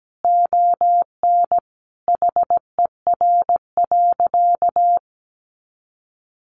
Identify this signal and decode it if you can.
Morse code, heard as a keyed tone, about 17 wpm, 700 Hz. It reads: ON HER.
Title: Sentences from Top 1300 Words 17wpm